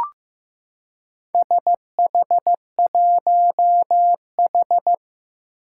Morse code, heard as a keyed tone, about 15 wpm, 700 Hz. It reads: SH1H
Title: Four Challenging Letter-Number Combinations 15wpm